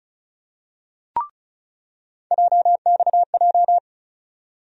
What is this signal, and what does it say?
Morse code, heard as a keyed tone, about 36 wpm, 700 Hz. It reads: JXJ
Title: USA Call Sign Postfixes - 3 Letters 35wpm